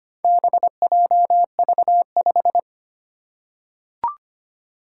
Morse code, heard as a keyed tone, about 25 wpm, 700 Hz. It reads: BJV5
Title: Four Challenging Letter-Number Combinations 25wpm